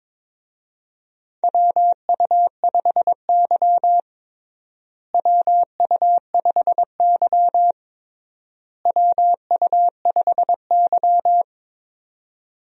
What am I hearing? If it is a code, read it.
Morse code, heard as a keyed tone, about 22 wpm, 700 Hz. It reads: WU5Y WU5Y WU5Y
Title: USA Call Signs - 4 Characters - Repeated 3x - 22wpm